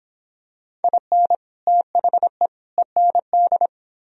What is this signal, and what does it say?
Morse code, heard as a keyed tone, about 26 wpm, 700 Hz. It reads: IN THE END